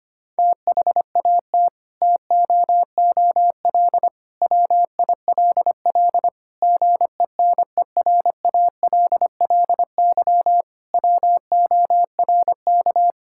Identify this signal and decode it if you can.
Morse code, heard as a keyed tone, about 25 wpm, 700 Hz. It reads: THAT TOOL WILL GENERALLY WORK
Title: Sentences from Top 1000 Words - Review Part 6 - 25wpm